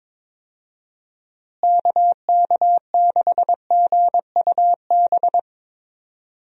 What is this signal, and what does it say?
Morse code, heard as a keyed tone, about 22 wpm, 700 Hz. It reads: KK6GUB
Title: USA Call Signs - 6 Characters - Repeated 3x - 22wpm